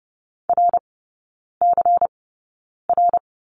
Morse code, heard as a keyed tone, about 30 wpm, spaced as for 9 wpm, 690 Hz. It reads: RCR
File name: Lesson 16e - Letter c - Three Characters 30wpm